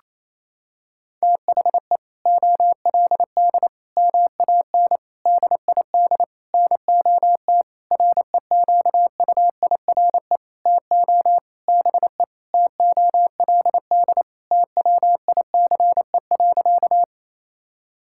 Morse code, heard as a keyed tone, about 28 wpm, 700 Hz. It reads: THE OLD MAN DID NOT REQUIRE TO BE TOLD TWICE.
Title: Sentences from Top 1600 Words 28wpm